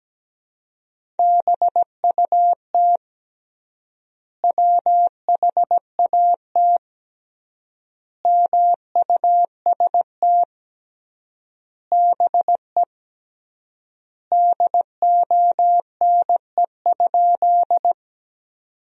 Morse code, heard as a keyed tone, about 17 wpm, 700 Hz. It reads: BUT WHAT MUST BE DONE?
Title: 3x Word-Spacing - Sentences from Top 500 Words - 17wpm